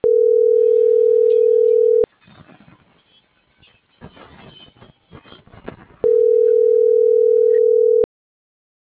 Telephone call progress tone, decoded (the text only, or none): ringback tone